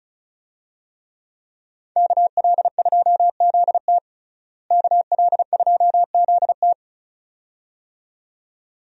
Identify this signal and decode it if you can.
Morse code, heard as a keyed tone, about 34 wpm, 700 Hz. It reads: KL2ZT KL2ZT
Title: USA Call Signs - 5 Characters - Repeated 2x - 35wpm